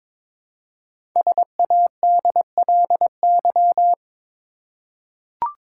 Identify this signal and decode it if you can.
Morse code, heard as a keyed tone, about 22 wpm, 700 Hz. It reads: SADLY
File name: 5 Letter Words - First 3 Letters Given - 22wpm